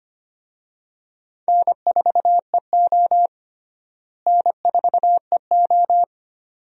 Morse code, heard as a keyed tone, about 25 wpm, 700 Hz. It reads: N4EO N4EO
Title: USA Call Signs - 4 Characters - Repeated 2x - 25wpm